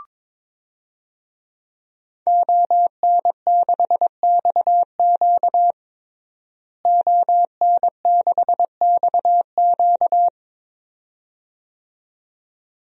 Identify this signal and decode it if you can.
Morse code, heard as a keyed tone, about 22 wpm, 700 Hz. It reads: ON6XQ ON6XQ
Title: DXCC-100 Call Signs - 5 Characters - Repeated 2x - 22wpm